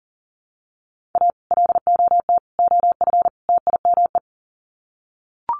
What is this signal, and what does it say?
Morse code, heard as a keyed tone, about 37 wpm, 700 Hz. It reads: A LOT OF TIME
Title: Sets of 4 Words 40wpm